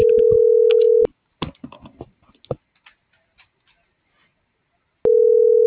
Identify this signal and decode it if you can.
Ringback tone (North America), 440 + 480 Hz.